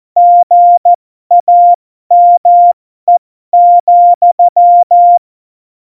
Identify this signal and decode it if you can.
Morse code, heard as a keyed tone, about 14 wpm, spaced as for 12 wpm, 700 Hz. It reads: GAME,
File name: Morsekurs / Witze